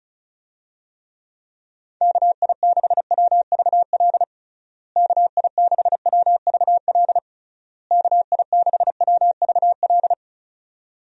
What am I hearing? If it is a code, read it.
Morse code, heard as a keyed tone, about 36 wpm, 700 Hz. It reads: KI6WVL KI6WVL KI6WVL
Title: USA Call Signs - 6 Characters - Repeated 3x - 35wpm